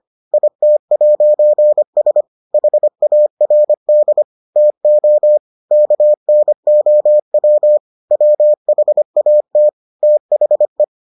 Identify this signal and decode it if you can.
Morse code, heard as a keyed tone, about 25 wpm, 600 Hz. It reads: IT'S HARD TO KNOW WHAT THE